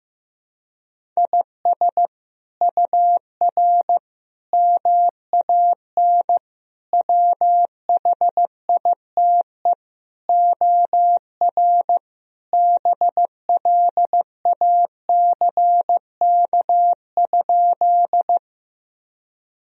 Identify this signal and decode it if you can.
Morse code, heard as a keyed tone, about 15 wpm, 700 Hz. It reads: IS UR MAN WHITE OR BLACK?
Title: Sentences from Top 400 Words and Abbreviations 15wpm